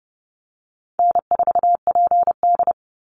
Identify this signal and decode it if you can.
Morse code, heard as a keyed tone, about 30 wpm, 700 Hz. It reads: N4PD
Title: Easy Call Signs - 30wpm